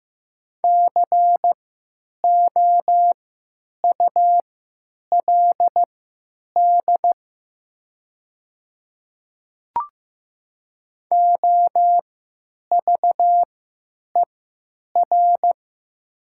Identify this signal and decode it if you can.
Morse code, heard as a keyed tone, about 15 wpm, spaced as for 9 wpm, 700 Hz. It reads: COULD OVER
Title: 3x Character Spacing - Top 100 Words - 15wpm